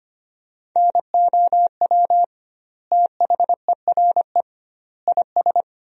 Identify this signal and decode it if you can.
Morse code, heard as a keyed tone, about 25 wpm, 700 Hz. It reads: NOW THERE IS
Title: 2x Word-Spacing - Sentences from Top 100 Words - 25wpm